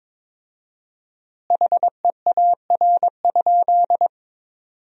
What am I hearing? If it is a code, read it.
Morse code, heard as a keyed tone, about 22 wpm, 700 Hz. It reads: HEAR?
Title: 4x Word-Spacing - Sentences from Top 300 Words - 22wpm